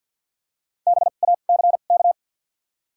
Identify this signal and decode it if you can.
Morse code, heard as a keyed tone, about 54 wpm, spaced as for 37 wpm, 700 Hz. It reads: BACK
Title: 2x Character Spacing - Top 300 Words - 50wpm